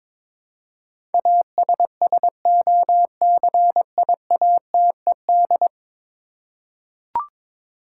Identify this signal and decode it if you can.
Morse code, heard as a keyed tone, about 22 wpm, 700 Hz. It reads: ASSOCIATED
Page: morse-code-ninja-practice-22wpm